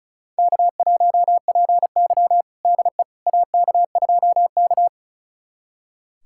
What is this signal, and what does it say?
Morse code, heard as a keyed tone, about 34 wpm, 700 Hz. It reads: K1PY DE AK2K
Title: USA Callsign DE Callsign - 4 Characters - 35wpm